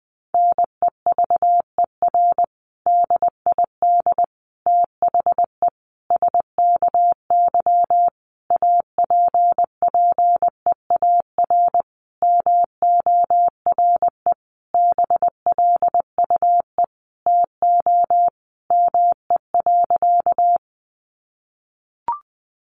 Morse code, heard as a keyed tone, about 20 wpm, 700 Hz. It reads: NEVER DID THE SKY APPEAR MORE BLUE TO ME.
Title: Sentences from Top 1200 Words 20wpm